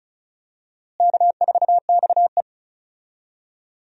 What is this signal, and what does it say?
Morse code, heard as a keyed tone, about 36 wpm, 700 Hz. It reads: K4XE
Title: USA Call Signs - 4 Characters - Repeated 2x - 35wpm